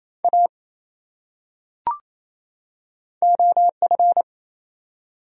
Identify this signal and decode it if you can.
Morse code, heard as a keyed tone, about 28 wpm, 700 Hz. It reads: A OF
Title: Top 100 Words 28wpm